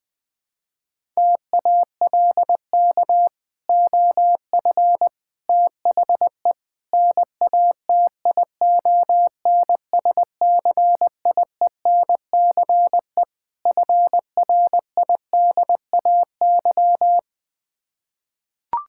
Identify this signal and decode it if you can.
Morse code, heard as a keyed tone, about 20 wpm, 700 Hz. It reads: TALK OF THE NATION SCIENCE FRIDAY
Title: Build A Phrase - 5-Word Phrases - 20wpm